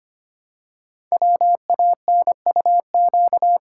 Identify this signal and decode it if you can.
Morse code, heard as a keyed tone, about 25 wpm, 700 Hz. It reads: WANUQ
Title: Code Groups - 5 Letter:Numbers - 25wpm